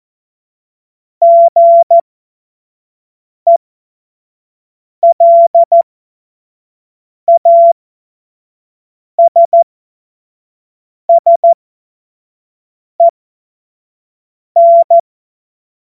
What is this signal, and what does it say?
Morse code, heard as a keyed tone, about 14 wpm, spaced as for 5 wpm, 680 Hz. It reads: GELASSEN
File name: Morsekurs / Witze / Witz 16 April 2024 25 F70